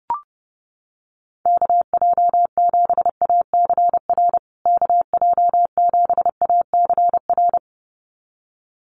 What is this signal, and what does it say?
Morse code, heard as a keyed tone, about 30 wpm, 700 Hz. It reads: KJ7ACR KJ7ACR
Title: USA Call Signs Repeated 2x 30wpm